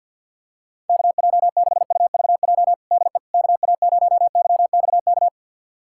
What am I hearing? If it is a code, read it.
Morse code, heard as a keyed tone, about 48 wpm, 700 Hz. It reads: KJ6UVJ DE KA0YXK
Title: USA Callsign DE Callsign - 6 Characters - 50wpm